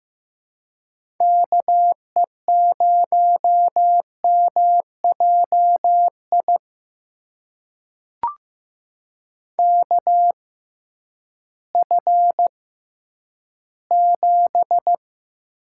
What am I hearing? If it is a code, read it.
Morse code, heard as a keyed tone, about 15 wpm, spaced as for 10 wpm, 700 Hz. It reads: KE0MJI KF7
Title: USA Call Signs - 6 Characters - 6x Character Spacing - 15wpm